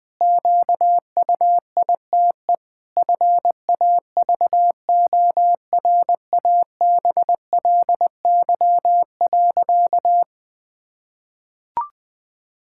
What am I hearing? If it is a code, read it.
Morse code, heard as a keyed tone, about 20 wpm, 700 Hz. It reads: QUITE FAVORABLY.
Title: Pangram Challenge v2 - 20wpm